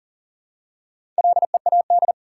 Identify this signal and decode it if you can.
Morse code, heard as a keyed tone, about 40 wpm, 700 Hz. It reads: LEAD
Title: Four Letter Words - 40wpm